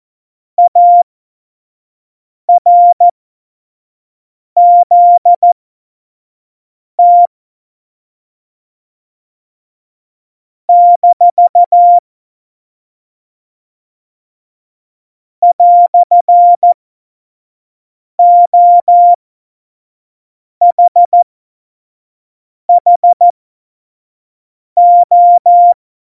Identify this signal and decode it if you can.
Morse code, heard as a keyed tone, about 14 wpm, spaced as for 5 wpm, 700 Hz. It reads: ARZT - "OHHO
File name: Morsekurs / Witze / Witz 16. Januar 2018, 25 F70